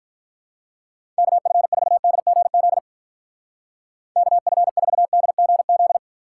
Morse code, heard as a keyed tone, about 44 wpm, 700 Hz. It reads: KF4DGZ KF4DGZ